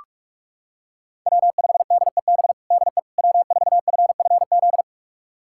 Morse code, heard as a keyed tone, about 46 wpm, 700 Hz. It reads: W5DEB DE W4FFZ
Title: USA Callsign DE Callsign - 5 Characters - 45wpm